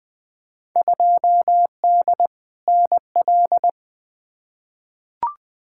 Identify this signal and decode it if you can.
Morse code, heard as a keyed tone, about 20 wpm, 700 Hz. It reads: 2D NL
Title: Field Day Exchanges Encore - 20wpm